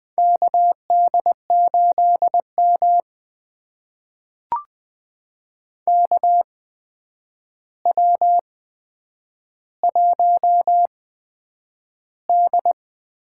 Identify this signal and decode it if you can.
Morse code, heard as a keyed tone, about 20 wpm, spaced as for 9 wpm, 700 Hz. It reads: KD8M KW1D
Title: USA Call Signs - 4 Characters - 8x Character Spacing - 20wpm